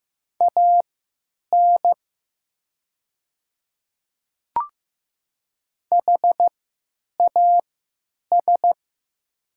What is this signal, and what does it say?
Morse code, heard as a keyed tone, about 15 wpm, spaced as for 9 wpm, 700 Hz. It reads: AN HAS
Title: 3x Character Spacing - Top 200 Words - 15wpm